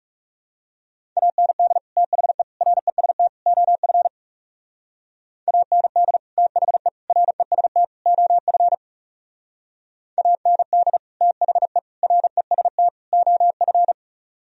Morse code, heard as a keyed tone, about 36 wpm, 700 Hz. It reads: AND THE REST OF AND THE REST OF AND THE REST OF
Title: Speed-Racing - Sets of 4 Words 30wpm